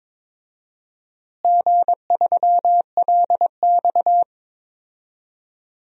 Morse code, heard as a keyed tone, about 21 wpm, 700 Hz. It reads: G3LX
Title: DXCC-100 Call Signs - 4 Characters - Repeated 3x - 22wpm